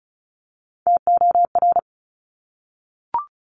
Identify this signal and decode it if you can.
Morse code, heard as a keyed tone, about 36 wpm, 700 Hz. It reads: TOR
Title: Top Three Letter Combinations in Words 35wpm